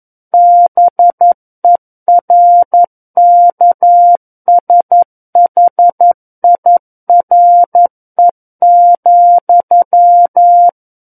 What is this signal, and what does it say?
Morse code, heard as a keyed tone, about 11 wpm, 700 Hz. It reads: BERKSHIRE,
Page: SKCC - War of the Worlds - 02-Chapter 2 - 11 WPM